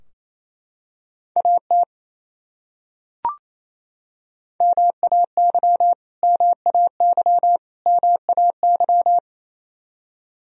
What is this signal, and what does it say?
Morse code, heard as a keyed tone, about 28 wpm, 700 Hz. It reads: AT MAY MAY MAY
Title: Top 100 Words Repeated 28wpm